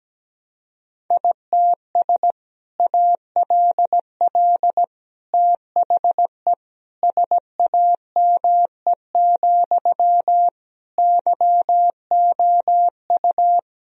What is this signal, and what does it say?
Morse code, heard as a keyed tone, about 17 wpm, 700 Hz. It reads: ITS ALL THE SAME, YOU
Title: Sentences from Top 1000 Words - Review Part 3 - 17wpm